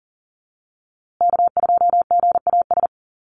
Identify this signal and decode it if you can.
Morse code, heard as a keyed tone, about 40 wpm, 700 Hz. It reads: K2GAS
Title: USA Call Signs Repeated 2x 40wpm